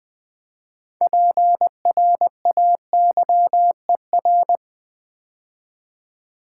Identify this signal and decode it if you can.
Morse code, heard as a keyed tone, about 20 wpm, 700 Hz. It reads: PRAYER
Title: 6 Letter Words - First 3 Letters Given - 20wpm